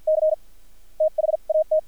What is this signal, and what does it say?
Morse code, heard as a keyed tone, about 50 wpm, spaced as for 41 wpm, 630 Hz. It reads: C THAT
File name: archived / CRSnet